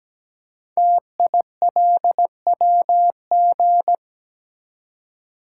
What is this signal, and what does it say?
Morse code, heard as a keyed tone, about 17 wpm, 700 Hz. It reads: TILWG
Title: Code Groups - 5 Letters - 17wpm